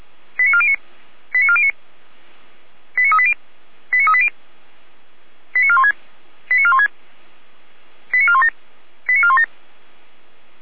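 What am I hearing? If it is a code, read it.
Five-tone selective call (ZVEI), heard as quick runs of five tones, tones of about 70 ms. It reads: 89409 89409 89380 89380 89416 89416 89417 89417
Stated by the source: ZVEI five-tone selective call received on an SDR by the Feuerwehr Orlamünde (fire brigade).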